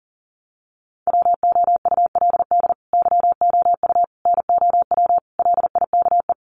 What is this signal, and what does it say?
Morse code, heard as a keyed tone, about 40 wpm, 700 Hz. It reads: WOULD YOU NOW LIKE
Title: Sentences from Top 200 Words 40wpm